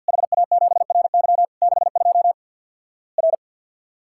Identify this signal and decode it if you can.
Morse code, heard as a keyed tone, about 50 wpm, 700 Hz. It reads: HA7RY 62 R
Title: Worked All Europe DX Contest - QTC Exchange - 50wpm